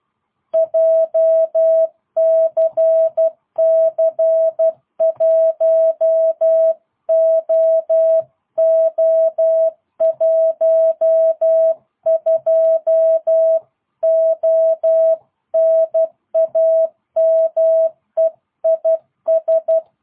Morse code, heard as a keyed tone, about 12 wpm, 660 Hz. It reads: JCC1OO12O NAME IS
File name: jh8cbh / 2022cw